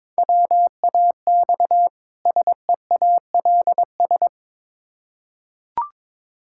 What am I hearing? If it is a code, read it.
Morse code, heard as a keyed tone, about 21 wpm, 700 Hz. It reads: WAX SEALS
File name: Difficult Words with Hints - 22wpm